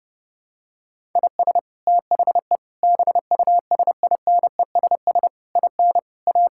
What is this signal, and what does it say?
Morse code, heard as a keyed tone, about 30 wpm, 700 Hz. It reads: IS THE BUSINESS IN A
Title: Sentences from Top 1000 Words - Review Part 5 - 30wpm